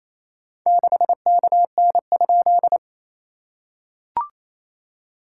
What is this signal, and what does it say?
Morse code, heard as a keyed tone, about 27 wpm, 700 Hz. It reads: 6KN?
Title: SOTA QSO Practice 28wpm